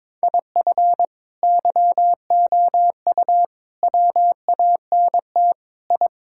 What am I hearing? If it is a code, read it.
Morse code, heard as a keyed tone, about 22 wpm, 700 Hz. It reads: IF YOU WANT I
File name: practice / 22wpm / Sentences with 3-Word Phrases - 22wpm